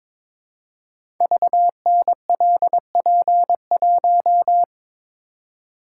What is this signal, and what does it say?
Morse code, heard as a keyed tone, about 22 wpm, 700 Hz. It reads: VNLP1
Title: Code Groups - 5 Letter:Numbers - 22wpm